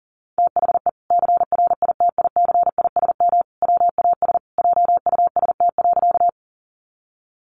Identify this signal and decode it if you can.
Morse code, heard as a keyed tone, about 40 wpm, 700 Hz. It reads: THE CRITICISM WAS JUST.
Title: Sentences from Top 1300 Words 40wpm